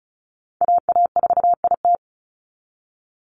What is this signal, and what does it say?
Morse code, heard as a keyed tone, about 34 wpm, 700 Hz. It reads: AA4IT
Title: NA SOTA Chaser Callsigns 35wpm